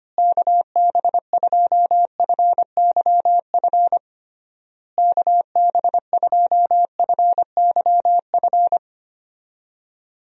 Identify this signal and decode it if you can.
Morse code, heard as a keyed tone, about 25 wpm, 700 Hz. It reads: KB2FYF KB2FYF
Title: USA Call Signs - 6 Characters - Repeated 3x - 25wpm